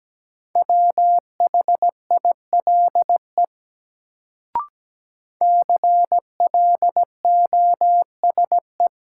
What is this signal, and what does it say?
Morse code, heard as a keyed tone, about 17 wpm, 700 Hz. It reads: WHILE CLOSE
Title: Sentences from Top 300 Words - New Word Drill - 17wpm